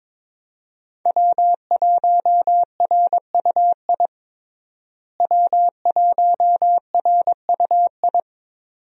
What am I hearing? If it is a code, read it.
Morse code, heard as a keyed tone, about 22 wpm, 700 Hz. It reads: W1RUI W1RUI
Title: USA Call Signs - 5 Characters - Repeated 2x - 22wpm